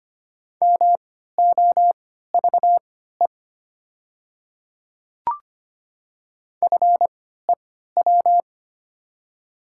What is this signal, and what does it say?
Morse code, heard as a keyed tone, about 25 wpm, spaced as for 14 wpm, 700 Hz. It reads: MOVE FEW
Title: 3x Character Spacing - Top 300 Words - 25wpm